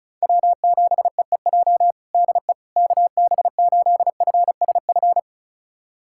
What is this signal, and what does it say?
Morse code, heard as a keyed tone, about 34 wpm, 690 Hz. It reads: W7EEJ DE KB8FSF
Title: USA Callsign DE Callsign 35wpm